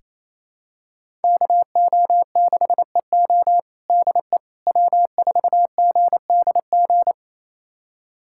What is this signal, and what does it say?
Morse code, heard as a keyed tone, about 28 wpm, 700 Hz. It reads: KO6EO DE W4GDG
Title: USA Callsign DE Callsign - 5 Characters - 28wpm